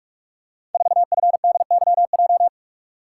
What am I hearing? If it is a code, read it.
Morse code, heard as a keyed tone, about 44 wpm, 700 Hz. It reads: 4FDYJ
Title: Code Groups - 5 Letter:Numbers - 45wpm